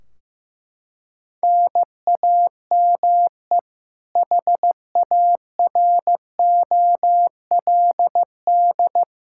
Morse code, heard as a keyed tone, about 15 wpm, 700 Hz. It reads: NAME HAROLD
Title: Practice QSO - Exchange 1 - Beginning - 15wpm